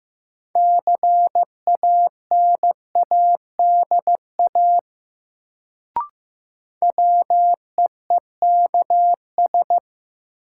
Morse code, heard as a keyed tone, about 15 wpm, 700 Hz. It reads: CANADA WEEKS
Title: Sets of 5 Words - New Word Drill - 15wpm